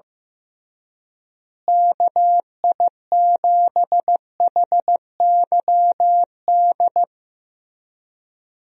Morse code, heard as a keyed tone, about 15 wpm, 700 Hz. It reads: KI7HYD
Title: USA Call Signs - 6 Characters - Repeated 3x - 15wpm